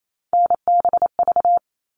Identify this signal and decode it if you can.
Morse code, heard as a keyed tone, about 28 wpm, 700 Hz. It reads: NBV
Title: All Cut Numbers - 3-Digit 28wpm